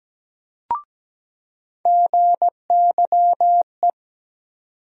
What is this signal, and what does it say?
Morse code, heard as a keyed tone, about 17 wpm, 700 Hz. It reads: GYE
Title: USA Call Sign Postfixes - 3 Letters 17wpm